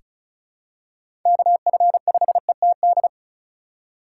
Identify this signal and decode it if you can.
Morse code, heard as a keyed tone, about 36 wpm, 700 Hz. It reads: KF5ETD
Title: K1USN SST QSO Practice - 35wpm